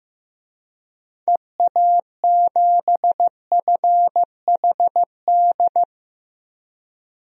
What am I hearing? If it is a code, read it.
Morse code, heard as a keyed tone, about 15 wpm, 700 Hz. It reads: EA7FHD
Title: DXCC-100 Call Signs - 6 Characters - 15wpm